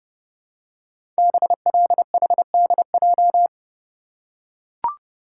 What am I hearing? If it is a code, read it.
Morse code, heard as a keyed tone, about 30 wpm, 700 Hz. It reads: BLHDJ
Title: Code Groups - 5 Letters - 30wpm